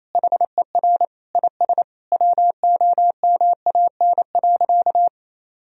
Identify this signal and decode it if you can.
Morse code, heard as a keyed tone, about 28 wpm, 700 Hz. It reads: HER IS WOMAN.
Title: Sentences from Top 1000 Words - Review Part 3 - 28wpm